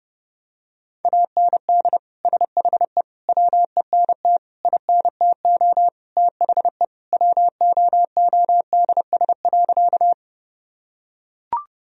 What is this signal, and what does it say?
Morse code, heard as a keyed tone, about 30 wpm, 700 Hz. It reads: AND SHE WENT INTO THE WOODS.
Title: Sentences from Top 1000 Words - Review Part 2 - 30wpm